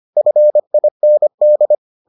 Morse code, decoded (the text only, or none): FIND